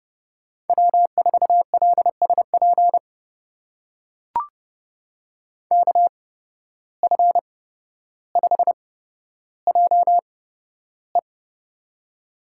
Morse code, heard as a keyed tone, about 30 wpm, spaced as for 13 wpm, 700 Hz. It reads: W4LSP KF5JE
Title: USA Call Signs - 5 Characters - 8x Character Spacing - 30wpm